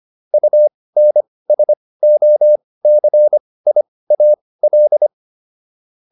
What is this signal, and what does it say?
Morse code, heard as a keyed tone, about 25 wpm, spaced as for 18 wpm, 600 Hz. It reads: UNSOCIAL